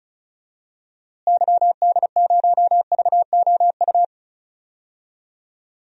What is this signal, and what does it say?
Morse code, heard as a keyed tone, about 34 wpm, 700 Hz. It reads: YD0VOU
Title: DXCC-100 Call Signs - 6 Characters - Repeated 3x - 35wpm